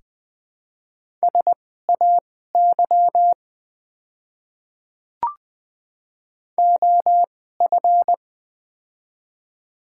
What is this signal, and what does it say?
Morse code, heard as a keyed tone, about 20 wpm, spaced as for 14 wpm, 700 Hz. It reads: SAY OF
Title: 2x Character Spacing - Top 200 Words - 20wpm